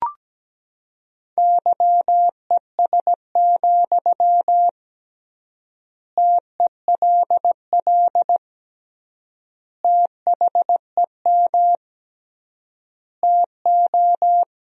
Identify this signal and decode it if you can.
Morse code, heard as a keyed tone, about 17 wpm, 700 Hz. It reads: YES, TELL THEM TO
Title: 3x Word-Spacing - Sentences from Top 500 Words - 17wpm